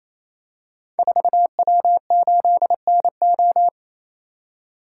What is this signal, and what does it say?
Morse code, heard as a keyed tone, about 28 wpm, 700 Hz. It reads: 4W8NO
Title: Code Groups - 5 Letter:Numbers - 28wpm